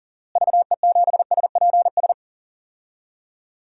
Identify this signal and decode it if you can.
Morse code, heard as a keyed tone, about 40 wpm, 700 Hz. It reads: VE7SPS
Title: DXCC-100 Call Signs - 6 Characters - Repeated 2x - 40wpm